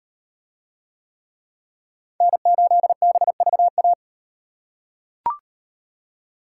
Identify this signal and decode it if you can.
Morse code, heard as a keyed tone, about 38 wpm, 700 Hz. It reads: N8BVA